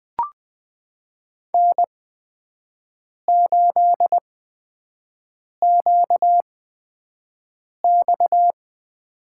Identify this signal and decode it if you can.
Morse code, heard as a keyed tone, about 20 wpm, spaced as for 5 wpm, 700 Hz. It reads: N8QX
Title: USA Call Signs - 4 Characters - 8x Character Spacing - 20wpm